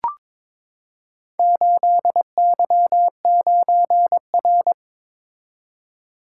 Morse code, heard as a keyed tone, about 22 wpm, 700 Hz. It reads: 8Y9R
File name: practice / 22wpm / Four Letter-Number Combinations - 22wpm